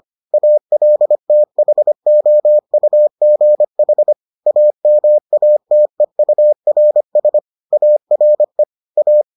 Morse code, heard as a keyed tone, about 25 wpm, 600 Hz. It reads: ALTHOUGH AMATEURS ARE A